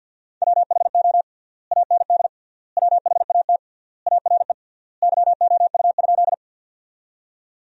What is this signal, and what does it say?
Morse code, heard as a keyed tone, about 50 wpm, 700 Hz. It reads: WHO AND WHAT ARE YOU?
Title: 3x Word-Spacing - Sentences from Top 100 Words - 50wpm